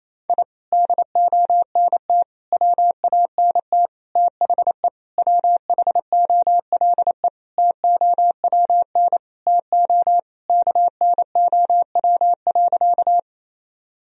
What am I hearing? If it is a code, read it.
Morse code, heard as a keyed tone, about 27 wpm, 700 Hz. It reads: I DONT WANT THE WHOLE TOWN TO KNOW.
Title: Sentences from Top 1000 Words - Review Part 2 - 28wpm